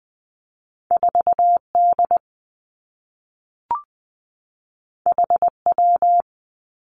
Morse code, heard as a keyed tone, about 20 wpm, 700 Hz. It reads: 4 D H W
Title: Two Challenging Letter-Number Combinations 20wpm